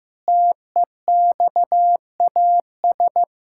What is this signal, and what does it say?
Morse code, heard as a keyed tone, about 15 wpm, 700 Hz. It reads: TEXAS
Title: Pangram Challenge v2 - 15wpm